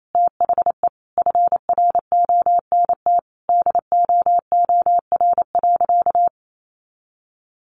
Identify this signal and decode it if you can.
Morse code, heard as a keyed tone, about 28 wpm, 700 Hz. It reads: THE FRONT DOOR.
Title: Sentences from Top 1300 Words 28wpm